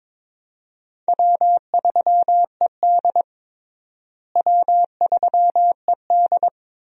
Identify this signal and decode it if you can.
Morse code, heard as a keyed tone, about 22 wpm, 700 Hz. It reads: W3ED W3ED
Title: USA Call Signs - 4 Characters - Repeated 2x - 22wpm